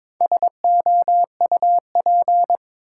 Morse code, heard as a keyed tone, about 22 wpm, 700 Hz. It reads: SOUP
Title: Sentences from Top 1600 Words - New Word Drill Alt - 22wpm